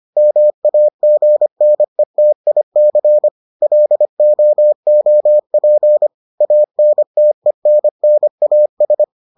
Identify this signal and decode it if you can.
Morse code, heard as a keyed tone, about 25 wpm, 600 Hz. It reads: MAGNETIC LOOP ANTENNAS